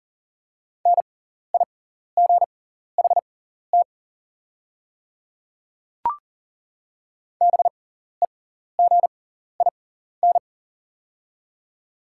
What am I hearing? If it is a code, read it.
Morse code, heard as a keyed tone, about 40 wpm, spaced as for 14 wpm, 700 Hz. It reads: NIGHT BEGIN
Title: 6x Character Spacing - Top 500 Words - 40wpm